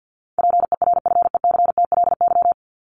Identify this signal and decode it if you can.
Morse code, heard as a keyed tone, about 50 wpm, 700 Hz. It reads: PERFECTLY
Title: LY Suffix - Top 100 Words with Base Words 50wpm